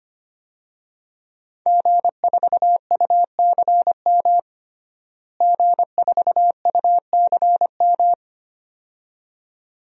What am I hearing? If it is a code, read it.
Morse code, heard as a keyed tone, about 25 wpm, 700 Hz. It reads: G4UCM G4UCM
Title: DXCC-100 Call Signs - 5 Characters - Repeated 2x - 25wpm